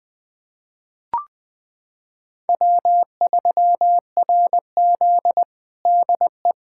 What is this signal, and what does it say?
Morse code, heard as a keyed tone, about 20 wpm, 700 Hz. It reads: W3RZ DE
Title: USA Callsign DE Callsign - 4 Characters - 20wpm